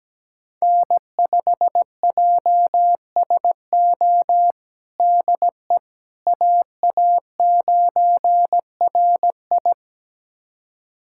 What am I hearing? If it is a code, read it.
Morse code, heard as a keyed tone, about 17 wpm, 700 Hz. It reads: N5JSO DE AA9RI
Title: USA Callsign DE Callsign - 5 Characters - 17wpm